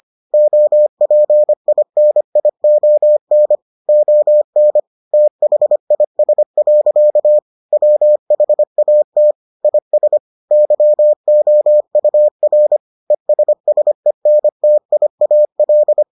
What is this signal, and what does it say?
Morse code, heard as a keyed tone, about 25 wpm, 600 Hz. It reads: OPINION ON THIS. WHAT IS YOUR ESSENTIAL